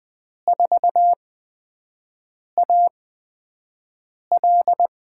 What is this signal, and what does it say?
Morse code, heard as a keyed tone, about 20 wpm, spaced as for 5 wpm, 700 Hz. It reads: 4AL
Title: USA Call Signs - 6 Characters - 8x Character Spacing - 20wpm